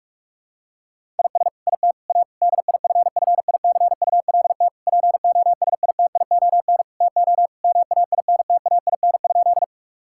Morse code, heard as a keyed tone, about 44 wpm, 700 Hz. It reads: IS IT A DIFFICULT POSITION TO MAINTAIN?
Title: Sentences from Top 1000 Words - Review Part 5 - 45wpm